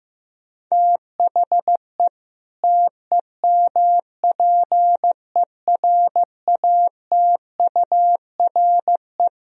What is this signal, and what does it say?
Morse code, heard as a keyed tone, about 15 wpm, 700 Hz. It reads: THE TEMPERATURE
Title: Sentences from Top 1000 Words - Review Part 4 - 15wpm